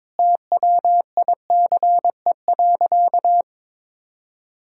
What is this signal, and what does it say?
Morse code, heard as a keyed tone, about 21 wpm, 700 Hz. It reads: TWICE.
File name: practice / 22wpm / Sentences with 3-Word Phrases - 22wpm